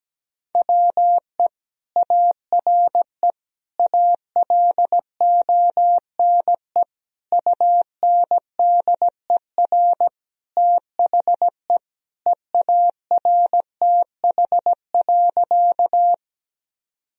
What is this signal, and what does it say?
Morse code, heard as a keyed tone, about 17 wpm, 700 Hz. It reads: WE ARE ALONE UNDER THE EARTH.
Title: Sentences from Top 1000 Words - Review Part 1 - 17wpm